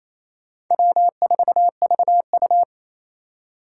Morse code, heard as a keyed tone, about 35 wpm, 700 Hz. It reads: W4VU